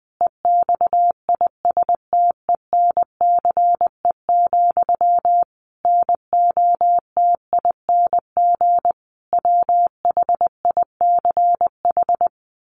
Morse code, heard as a keyed tone, about 20 wpm, 700 Hz. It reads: EXISTENCE, NOTING WHICH
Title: A Sound Of Thunder 20wpm